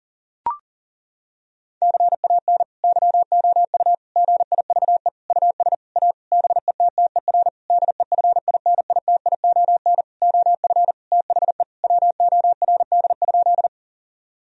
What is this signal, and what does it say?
Morse code, heard as a keyed tone, about 40 wpm, 700 Hz. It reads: CAN YOU GIVE US A BETTER DEFINITION OF THE WORD?
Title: Sentences from Top 1000 Words - Review Part 4 - 40wpm